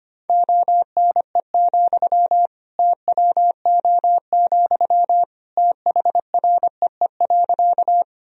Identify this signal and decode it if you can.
Morse code, heard as a keyed tone, about 25 wpm, 700 Hz. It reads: ONE, TWO, THREE.
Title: Sentences from Top 1000 Words - Review Part 1 - 25wpm